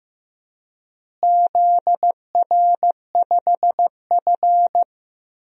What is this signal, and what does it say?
Morse code, heard as a keyed tone, about 15 wpm, 700 Hz. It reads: ZR5F
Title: DXCC-100 Call Signs - 4 Characters - Repeated 2x - 15wpm